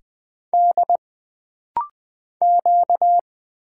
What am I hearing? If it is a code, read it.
Morse code, heard as a keyed tone, about 20 wpm, 700 Hz. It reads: D Q
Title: Single Letters - Rapid-Fire 20wpm